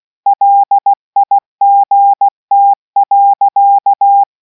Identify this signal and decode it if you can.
Morse code, heard as a keyed tone, about 16 wpm, 800 Hz. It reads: LIGT.